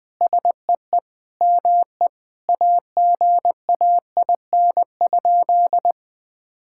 Morse code, heard as a keyed tone, about 20 wpm, 700 Hz. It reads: SEE ME AGAIN?
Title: Sentences from Top 1600 Words 20wpm